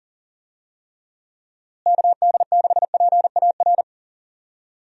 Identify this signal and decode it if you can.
Morse code, heard as a keyed tone, about 40 wpm, 700 Hz. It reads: KD6PAR
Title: USA Call Signs - 6 Characters - Repeated 3x - 40wpm